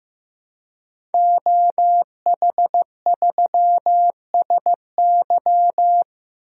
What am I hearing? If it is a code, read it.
Morse code, heard as a keyed tone, about 15 wpm, 700 Hz. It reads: OH3SY
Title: DXCC-100 Call Signs - 5 Characters - Repeated 2x - 15wpm